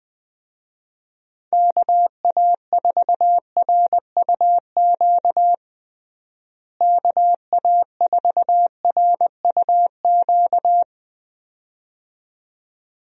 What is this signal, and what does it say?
Morse code, heard as a keyed tone, about 20 wpm, 700 Hz. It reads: KA4RUQ KA4RUQ
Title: USA Call Signs - 6 Characters - Repeated 2x - 20wpm